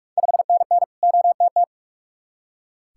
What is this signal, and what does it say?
Morse code, heard as a keyed tone, about 44 wpm, 690 Hz. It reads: 5NN OTT
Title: Ontario QSO Party Exchange 45wpm